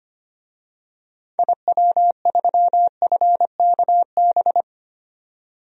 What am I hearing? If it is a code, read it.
Morse code, heard as a keyed tone, about 25 wpm, 700 Hz. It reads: IW3FKB
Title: DXCC-100 Call Signs - 6 Characters - 25wpm